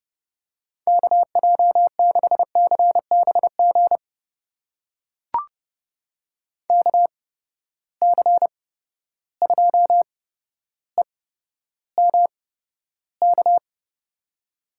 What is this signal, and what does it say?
Morse code, heard as a keyed tone, about 30 wpm, spaced as for 13 wpm, 700 Hz. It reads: KJ6CBG KC2EMK
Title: USA Call Signs - 6 Characters - 8x Character Spacing - 30wpm